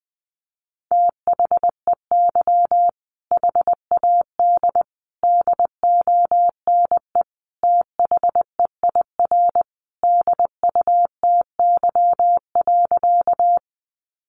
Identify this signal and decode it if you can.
Morse code, heard as a keyed tone, about 20 wpm, 700 Hz. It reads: THEY HAD DONE THEIR DUTY.
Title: Sentences from Top 1500 Words 20wpm